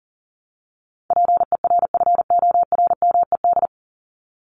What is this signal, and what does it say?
Morse code, heard as a keyed tone, about 40 wpm, 700 Hz. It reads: PERFORMED
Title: ED Suffix - Top 100 Words with Base Words 40wpm